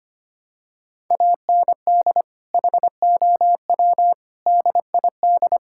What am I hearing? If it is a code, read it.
Morse code, heard as a keyed tone, about 25 wpm, 700 Hz. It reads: AND HOW DID
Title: Sentences from Top 1000 Words - Review Part 6 - 25wpm